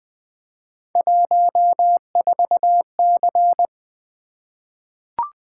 Morse code, heard as a keyed tone, about 20 wpm, 700 Hz. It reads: 14C
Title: Three Letter-Number Combinations - 20wpm